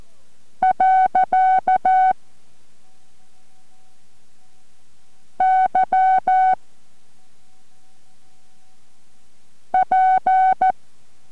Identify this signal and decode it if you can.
Morse code, heard as a keyed tone, about 14 wpm, 750 Hz. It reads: . Y P